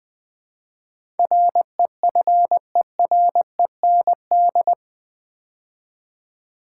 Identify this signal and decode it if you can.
Morse code, heard as a keyed tone, about 20 wpm, 700 Hz. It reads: REFEREND
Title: Build A Word - 4 Letters At A Time - 10-Letter Words - 20wpm